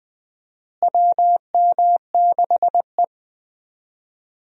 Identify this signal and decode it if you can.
Morse code, heard as a keyed tone, about 20 wpm, 700 Hz. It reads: WM6E
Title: USA Call Signs - 4 Characters - Repeated 2x - 20wpm